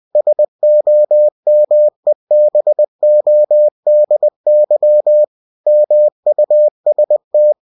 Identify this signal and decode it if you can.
Morse code, heard as a keyed tone, about 20 wpm, 600 Hz. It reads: SOMEBODY MUST
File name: cw-books / winnie-the-pooh / winnie-20wpm-page-0027